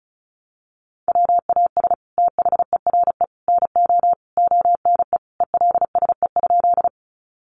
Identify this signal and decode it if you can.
Morse code, heard as a keyed tone, about 34 wpm, 700 Hz. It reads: WAS THERE NO ONE ELSE?
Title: Sentences from Top 800 Words 35wpm